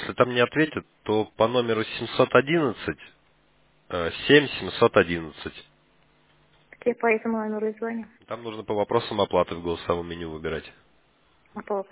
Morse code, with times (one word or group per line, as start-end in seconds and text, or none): none